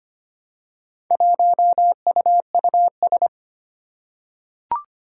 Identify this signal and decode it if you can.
Morse code, heard as a keyed tone, about 25 wpm, 700 Hz. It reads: 1UUS
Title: Four Challenging Letter-Number Combinations 25wpm